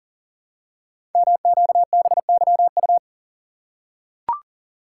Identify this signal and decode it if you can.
Morse code, heard as a keyed tone, about 38 wpm, 700 Hz. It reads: MQBYU
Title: Code Groups - 5 Letters - 40wpm